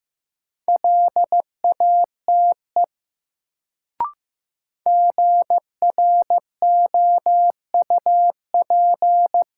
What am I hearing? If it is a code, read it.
Morse code, heard as a keyed tone, about 15 wpm, 700 Hz. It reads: LATE GROUP
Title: Sets of 3 Words - New Word Drill - 15wpm